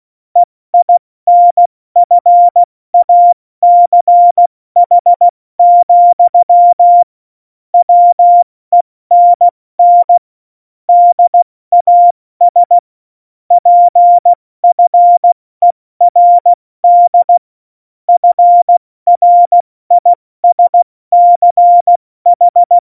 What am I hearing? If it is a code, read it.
Morse code, heard as a keyed tone, about 16 wpm, spaced as for 14 wpm, 700 Hz. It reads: EINFACH, WENN DAS PFERD FRISCH
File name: Morsekurs / Witze / Witz 03.Dezember 2019, 70 F80